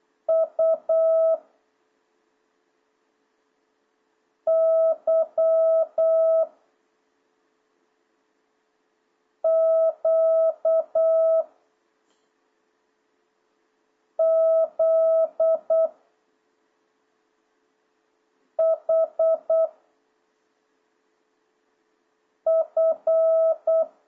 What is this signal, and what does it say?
Morse code, heard as a keyed tone, about 8 wpm, 650 Hz. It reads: U Y Q Z H F